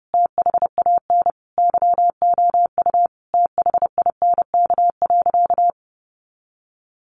Morse code, heard as a keyed tone, about 30 wpm, 700 Hz. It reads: THAN YOU THINK.
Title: Sentences from Top 400 Words 30wpm